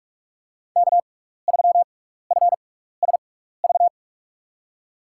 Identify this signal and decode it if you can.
Morse code, heard as a keyed tone, about 44 wpm, spaced as for 15 wpm, 700 Hz. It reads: K3FSV
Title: USA Call Signs - 5 Characters - 6x Character Spacing - 45wpm